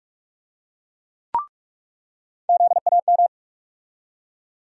Morse code, heard as a keyed tone, about 46 wpm, 700 Hz. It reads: ZAM
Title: USA Call Sign Postfixes - 3 Letters 45wpm